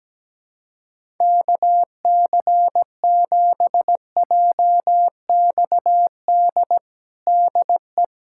Morse code, heard as a keyed tone, about 17 wpm, 700 Hz. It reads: KC7JXD DE
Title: USA Callsign DE Callsign - 6 Characters - 17wpm